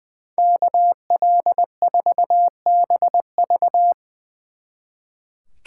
Morse code, heard as a keyed tone, about 21 wpm, 700 Hz. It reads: KL4BV
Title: Speed Racing - USA 2x2 Call Signs 20wpm